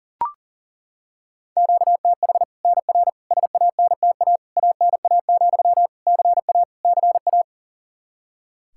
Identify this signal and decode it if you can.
Morse code, heard as a keyed tone, about 40 wpm, 700 Hz. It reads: QTH NR SANTA ANA, CA CA
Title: Practice QSO - Exchange 1 - Beginning - 40wpm